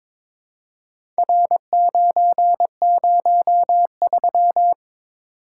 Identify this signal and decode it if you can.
Morse code, heard as a keyed tone, about 22 wpm, 700 Hz. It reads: R903
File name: practice / 22wpm / Four Letter-Number Combinations - 22wpm